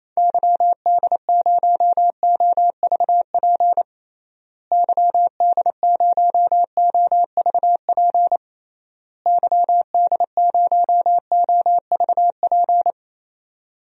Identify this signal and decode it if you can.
Morse code, heard as a keyed tone, about 28 wpm, 700 Hz. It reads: YD0OVP YD0OVP YD0OVP
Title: DXCC-100 Call Signs - 6 Characters - Repeated 3x - 28wpm